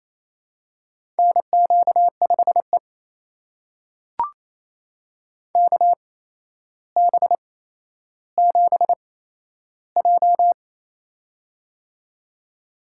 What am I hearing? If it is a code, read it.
Morse code, heard as a keyed tone, about 27 wpm, spaced as for 12 wpm, 700 Hz. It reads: NQ5E KB7J
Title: USA Call Signs - 4 Characters - 8x Character Spacing - 28wpm